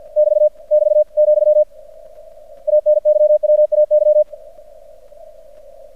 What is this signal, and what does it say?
Morse code, heard as a keyed tone, about 50 wpm, 600 Hz. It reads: <BT> <BT> , ANYWAY
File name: archived / CRSnet